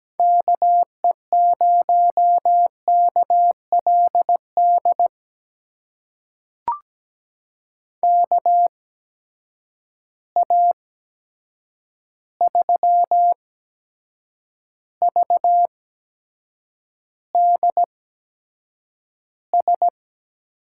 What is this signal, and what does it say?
Morse code, heard as a keyed tone, about 17 wpm, spaced as for 7 wpm, 700 Hz. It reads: KE0KLD KA3VDS
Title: USA Call Signs - 6 Characters - 8x Character Spacing - 17wpm